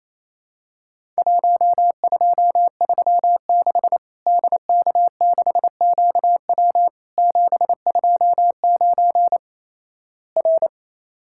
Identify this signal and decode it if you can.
Morse code, heard as a keyed tone, about 27 wpm, 700 Hz. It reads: 1236 DK6QW 729 R
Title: Worked All Europe DX Contest - QTC Exchange - 28wpm